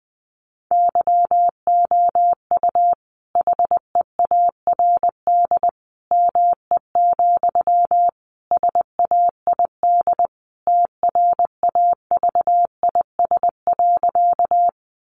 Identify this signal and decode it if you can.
Morse code, heard as a keyed tone, about 20 wpm, 700 Hz. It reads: YOU HEARD ME, SAID TRAVIS.
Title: A Sound Of Thunder 20wpm